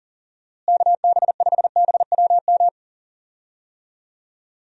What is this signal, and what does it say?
Morse code, heard as a keyed tone, about 40 wpm, 700 Hz. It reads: KB5BWM
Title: USA Call Signs - 6 Characters - Repeated 2x - 40wpm